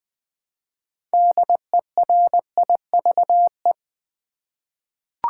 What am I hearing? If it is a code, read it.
Morse code, heard as a keyed tone, about 20 wpm, 700 Hz. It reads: DERIVE
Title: 6 Letter Words - First 3 Letters Given - 20wpm